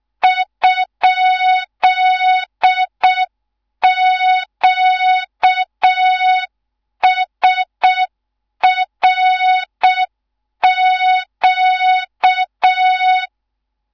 Morse code, heard as a keyed tone, about 6 wpm, 3700 Hz. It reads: ?QSRQ
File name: Morze-HA2MN / hang / 13IRASJELEK1 HA2MN